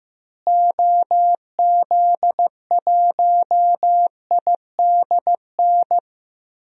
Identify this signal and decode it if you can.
Morse code, heard as a keyed tone, about 16 wpm, 700 Hz. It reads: OZ1IDN
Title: DXCC-100 Call Signs - 6 Characters - Repeated 3x - 15wpm